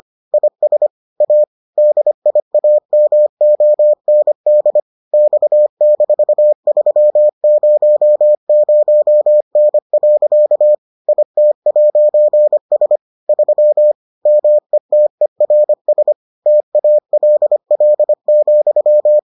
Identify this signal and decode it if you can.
Morse code, heard as a keyed tone, about 25 wpm, 600 Hz. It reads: IS A DIAMOND X-300N. IT'S 3 METERS TALL,